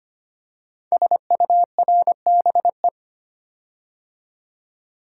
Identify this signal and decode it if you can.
Morse code, heard as a keyed tone, about 25 wpm, 700 Hz. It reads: SURBE
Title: Code Groups - 5 Letters - 25wpm